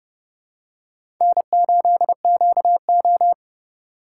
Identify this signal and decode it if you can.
Morse code, heard as a keyed tone, about 30 wpm, 700 Hz. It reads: N8QO
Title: USA Call Signs - 4 Characters - 3x Character Spacing - 30wpm